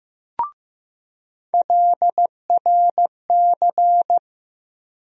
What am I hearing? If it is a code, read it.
Morse code, heard as a keyed tone, about 15 wpm, 700 Hz. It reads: LRC
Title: USA Call Sign Postfixes - 3 Letters 15wpm